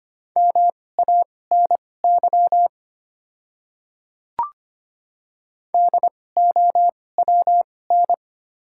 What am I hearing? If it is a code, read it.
Morse code, heard as a keyed tone, about 25 wpm, spaced as for 18 wpm, 690 Hz. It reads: MANY DOWN
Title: 2x Character Spacing - Top 100 Words - 25wpm